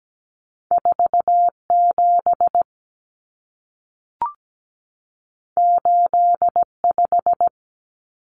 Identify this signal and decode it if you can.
Morse code, heard as a keyed tone, about 17 wpm, 700 Hz. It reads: 47 85
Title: Numbers - 2-Digit 17wpm